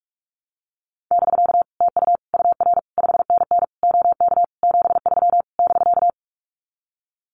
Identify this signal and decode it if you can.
Morse code, heard as a keyed tone, about 44 wpm, 700 Hz. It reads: <BK> TU UR 5NN OK 73 <BK>
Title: Common QSO Phrases Encore 45wpm